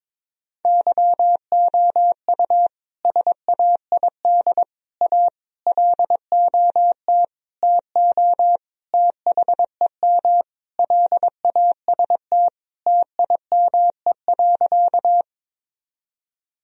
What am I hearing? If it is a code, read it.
Morse code, heard as a keyed tone, about 22 wpm, 700 Hz. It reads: YOU SAID A LOT TO THEM LAST TIME.
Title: Sentences from Top 1000 Words - Review Part 3 - 22wpm